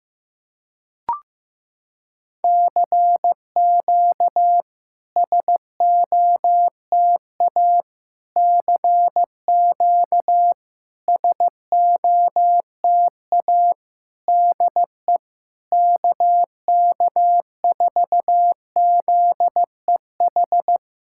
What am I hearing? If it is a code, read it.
Morse code, heard as a keyed tone, about 15 wpm, 700 Hz. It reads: CQ SOTA CQ SOTA DE KK4ZEH
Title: SOTA QSO Practice 15wpm